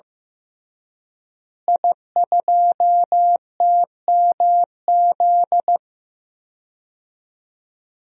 Morse code, heard as a keyed tone, about 15 wpm, 700 Hz. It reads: I2TMZ
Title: DXCC-100 Call Signs - 5 Characters - Repeated 3x - 15wpm